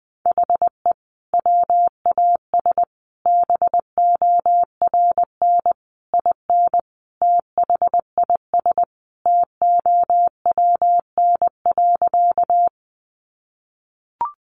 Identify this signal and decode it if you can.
Morse code, heard as a keyed tone, about 19 wpm, 700 Hz. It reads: HE WAS BORN IN THIS TOWN.
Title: Sentences from Top 1000 Words 20wpm